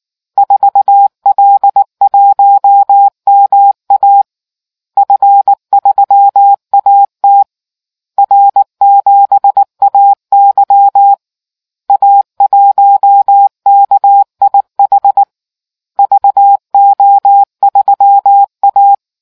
Morse code, heard as a keyed tone, about 19 wpm, 800 Hz. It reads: 4L1MA F3AT R7AY A1KIH VO3A